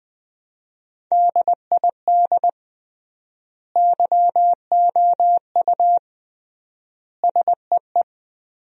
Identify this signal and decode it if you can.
Morse code, heard as a keyed tone, about 20 wpm, 700 Hz. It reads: DID YOU SEE
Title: 3x Word-Spacing - Sentences from Top 300 Words - 20wpm